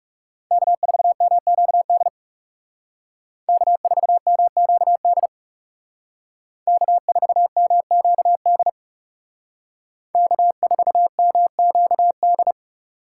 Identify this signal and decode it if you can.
Morse code, heard as a keyed tone, about 37 wpm, 700 Hz. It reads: K4MQD K4MQD K4MQD K4MQD
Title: Speed Racing - USA 1x3 Call Signs 30wpm